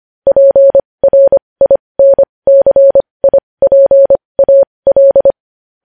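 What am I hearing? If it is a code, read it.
Morse code, heard as a keyed tone, about 25 wpm, spaced as for 20 wpm, 550 Hz. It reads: PRINCIPAL
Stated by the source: Morse code news headlines sent at 20 wpm.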